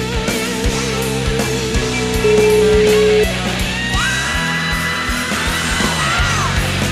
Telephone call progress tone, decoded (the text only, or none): ringback tone